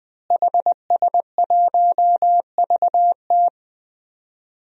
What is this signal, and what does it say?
Morse code, heard as a keyed tone, about 20 wpm, 700 Hz. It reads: HS1VT
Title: DXCC-100 Call Signs - 5 Characters - Repeated 3x - 20wpm